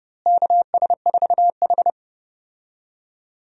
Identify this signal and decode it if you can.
Morse code, heard as a keyed tone, about 30 wpm, 700 Hz. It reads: KS4H
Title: USA Call Signs - 4 Characters - Repeated 3x - 30wpm